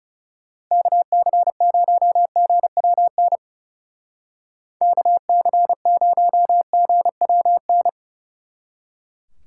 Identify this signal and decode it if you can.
Morse code, heard as a keyed tone, about 37 wpm, 700 Hz. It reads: KC0GWN KC0GWN
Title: Speed Racing - USA 2x3 Call Signs 30wpm